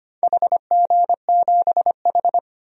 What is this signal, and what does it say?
Morse code, heard as a keyed tone, about 25 wpm, 700 Hz. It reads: HG7H
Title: DXCC-100 Call Signs - 4 Characters - 25wpm